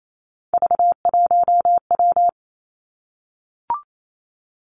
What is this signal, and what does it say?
Morse code, heard as a keyed tone, about 28 wpm, 700 Hz. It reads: V1W
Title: Three Challenging Letter-Number Combinations 28wpm